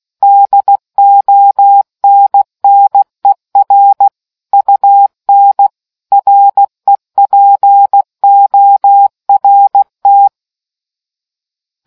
Morse code, heard as a keyed tone, about 16 wpm, 800 Hz. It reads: DONNER UN REPORT